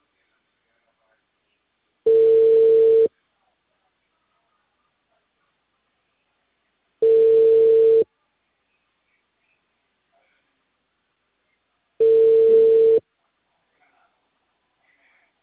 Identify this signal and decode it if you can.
Ringback tone (China, 450 Hz).